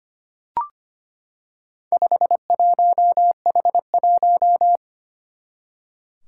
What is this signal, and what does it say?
Morse code, heard as a keyed tone, about 25 wpm, 700 Hz. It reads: 51H1
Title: Four Challenging Letter-Number Combinations 25wpm